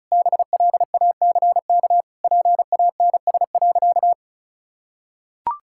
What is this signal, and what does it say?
Morse code, heard as a keyed tone, about 34 wpm, 700 Hz. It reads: BLACK PANS.
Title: Pangram Challenge v2 - 35wpm